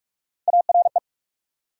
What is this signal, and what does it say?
Morse code, heard as a keyed tone, about 64 wpm, spaced as for 55 wpm, 690 Hz. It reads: ARE
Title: Speed-Racing Multi-letter Phonemes - 45wpm